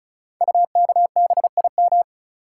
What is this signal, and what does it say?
Morse code, heard as a keyed tone, about 36 wpm, 700 Hz. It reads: UKBIM
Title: Code Groups - 5 Letters - 35wpm